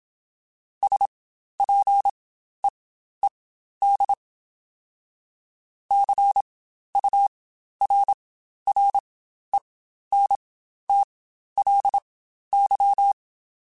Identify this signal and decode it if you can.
Morse code, heard as a keyed tone, about 27 wpm, spaced as for 12 wpm, 800 Hz. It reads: SPEED CURRENTLY
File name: Spaced Characters / Spaced Characters-mp3s / Weather5 5 sp @25